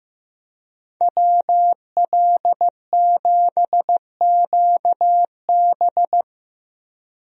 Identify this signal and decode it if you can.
Morse code, heard as a keyed tone, about 16 wpm, 700 Hz. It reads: WL7QB
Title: USA Call Signs - 5 Characters - Repeated 2x - 15wpm